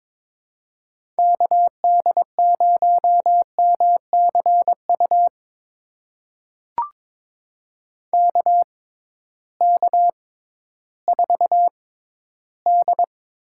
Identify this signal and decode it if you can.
Morse code, heard as a keyed tone, about 22 wpm, spaced as for 13 wpm, 700 Hz. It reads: KD0MCU KK4D
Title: USA Call Signs - 6 Characters - 6x Character Spacing - 22wpm